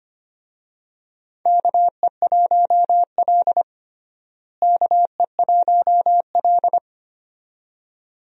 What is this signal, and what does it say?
Morse code, heard as a keyed tone, about 25 wpm, 700 Hz. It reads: KE1L KE1L
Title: USA Call Signs - 4 Characters - Repeated 2x - 25wpm